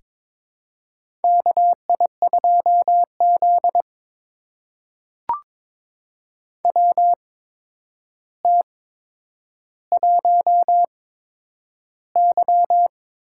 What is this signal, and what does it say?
Morse code, heard as a keyed tone, about 22 wpm, spaced as for 9 wpm, 700 Hz. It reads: KI2Z WT1Y
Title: USA Call Signs - 4 Characters - 8x Character Spacing - 22wpm